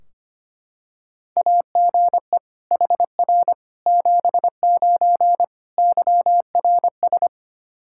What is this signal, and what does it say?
Morse code, heard as a keyed tone, about 25 wpm, 700 Hz. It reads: AGE HR 79 YRS
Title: Practice QSO - Exchange 3 - More Info - 25wpm